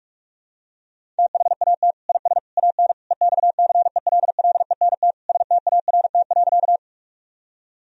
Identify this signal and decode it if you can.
Morse code, heard as a keyed tone, about 41 wpm, 700 Hz. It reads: THAT IS AN EXCELLENT START.
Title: Sentences from Top 1000 Words - Review Part 1 - 45wpm